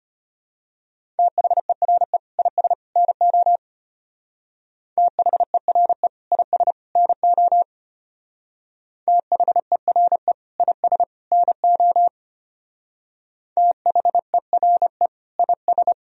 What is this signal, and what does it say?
Morse code, heard as a keyed tone, about 30 wpm, 700 Hz. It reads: THERE IS NO THERE IS NO THERE IS NO THERE IS
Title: Speed-Racing - Sets of 3 Words 25wpm